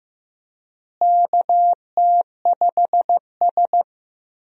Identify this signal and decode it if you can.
Morse code, heard as a keyed tone, about 14 wpm, 700 Hz. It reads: KT5S
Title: USA Call Signs - 4 Characters - Repeated 3x - 15wpm